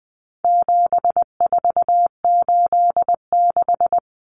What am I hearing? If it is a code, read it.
Morse code, heard as a keyed tone, about 20 wpm, 700 Hz. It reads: 7486
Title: Common Cut Numbers - 4-Digit 20wpm